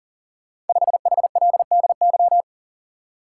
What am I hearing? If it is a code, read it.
Morse code, heard as a keyed tone, about 42 wpm, 690 Hz. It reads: 5HLDY
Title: Code Groups - 5 Letter:Numbers - 40wpm